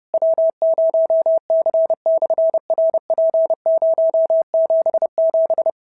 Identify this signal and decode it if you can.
Morse code, heard as a keyed tone, about 28 wpm, 650 Hz. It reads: W0C/RP077
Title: SOTA QSO Practice 30wpm